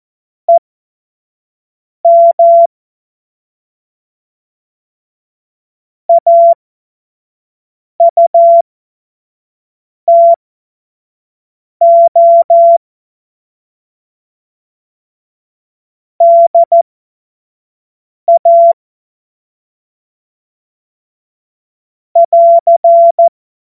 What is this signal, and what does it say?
Morse code, heard as a keyed tone, about 14 wpm, spaced as for 5 wpm, 680 Hz. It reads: EM AUTO DA <AR>
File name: Morsekurs / Witze / Witz 30 Januar 2024 25 F70